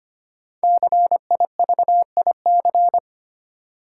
Morse code, heard as a keyed tone, about 25 wpm, 700 Hz. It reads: CIVIC
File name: 5 Letter Words - First 3 Letters Given - 25wpm